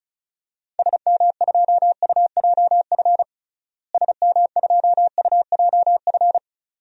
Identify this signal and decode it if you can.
Morse code, heard as a keyed tone, about 36 wpm, 700 Hz. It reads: SM2UJF SM2UJF
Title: DXCC-100 Call Signs - 6 Characters - Repeated 2x - 35wpm